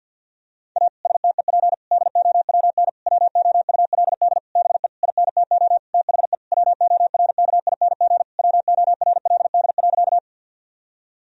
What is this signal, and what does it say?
Morse code, heard as a keyed tone, about 50 wpm, 700 Hz. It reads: A STEP DOWN WOULD BE INTO THE WORKING WORLD.
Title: Sentences from Top 1000 Words - Review Part 4 - 50wpm